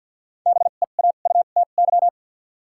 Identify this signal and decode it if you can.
Morse code, heard as a keyed tone, about 53 wpm, spaced as for 37 wpm, 700 Hz. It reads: BEAUTY
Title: 2x Character Spacing - Top 500 Words - 50wpm